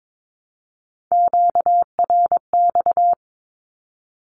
Morse code, heard as a keyed tone, about 22 wpm, 700 Hz. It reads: QRX
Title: Q-Codes Repeated 3x 22wpm